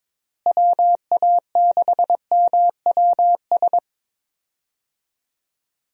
Morse code, heard as a keyed tone, about 22 wpm, 700 Hz. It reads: WA6MWS
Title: USA Call Signs - 6 Characters - Repeated 3x - 22wpm